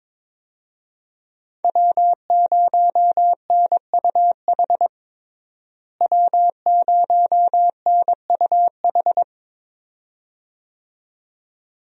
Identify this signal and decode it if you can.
Morse code, heard as a keyed tone, about 22 wpm, 700 Hz. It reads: W0NUH W0NUH
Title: USA Call Signs - 5 Characters - Repeated 2x - 22wpm